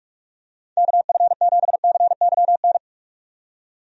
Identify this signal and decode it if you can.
Morse code, heard as a keyed tone, about 42 wpm, 700 Hz. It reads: KF7CYN
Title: USA Call Signs - 6 Characters - 45wpm